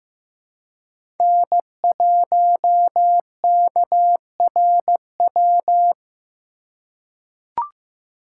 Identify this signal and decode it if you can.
Morse code, heard as a keyed tone, about 15 wpm, 700 Hz. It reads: N1KRW
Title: USA Call Signs - 5 Characters - 4x Character Spacing - 15wpm